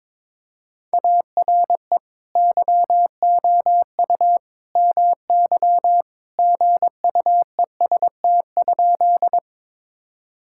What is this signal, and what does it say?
Morse code, heard as a keyed tone, about 22 wpm, 700 Hz. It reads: ARE YOU MY GUEST?
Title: Sentences from Top 1600 Words 22wpm